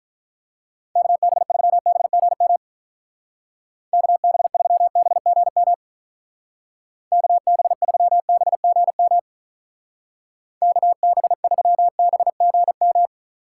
Morse code, heard as a keyed tone, about 44 wpm, 700 Hz. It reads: KB3BGM KB3BGM KB3BGM KB3BGM